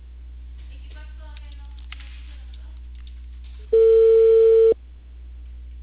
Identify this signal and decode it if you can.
Ringback tone (China, 450 Hz), with speech.